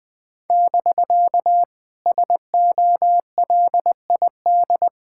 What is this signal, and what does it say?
Morse code, heard as a keyed tone, about 21 wpm, 700 Hz. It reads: <BK> SOLID
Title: Speed-Racing Common QSO Phrases - 20wpm